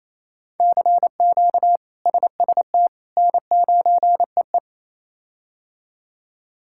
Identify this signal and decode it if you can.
Morse code, heard as a keyed tone, about 28 wpm, 700 Hz. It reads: CQ SST N9EE
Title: K1USN SST QSO Practice - 28wpm